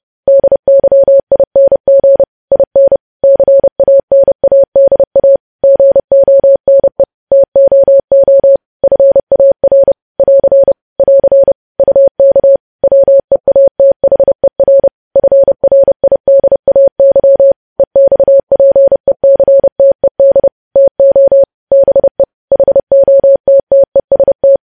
Morse code, heard as a keyed tone, about 30 wpm, 550 Hz. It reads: DYING IN CANADA GONE TOO FAR <AR> <AR> UK WEATHER FRIDAY EXPECTED TO BE HOTTEST
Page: News Headlines in Morse Code at 30 WPM